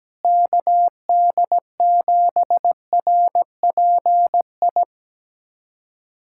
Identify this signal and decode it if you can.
Morse code, heard as a keyed tone, about 17 wpm, 700 Hz. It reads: KD7RPI
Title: USA Callsign DE Callsign 17wpm